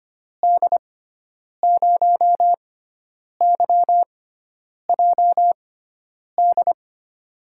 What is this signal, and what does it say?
Morse code, heard as a keyed tone, about 25 wpm, spaced as for 9 wpm, 700 Hz. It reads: D0YJD
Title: USA Call Signs - 6 Characters - 6x Character Spacing - 25wpm